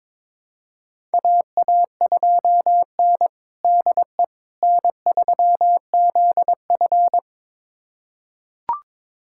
Morse code, heard as a keyed tone, about 22 wpm, 700 Hz. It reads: AA2N DE N3ZF
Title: USA Callsign DE Callsign - 4 Characters - 22wpm